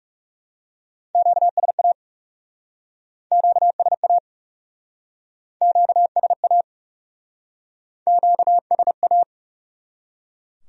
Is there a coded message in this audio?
Yes: Morse code from 1.2 to 9.2 s, heard as a keyed tone, about 40 wpm, 700 Hz. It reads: QSA QSA QSA QSA